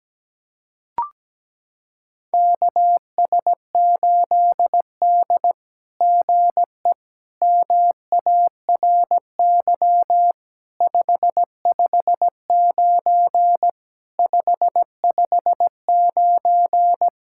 Morse code, heard as a keyed tone, about 17 wpm, 700 Hz. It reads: KS8D GE MARY 559 559
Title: SOTA QSO Practice 17wpm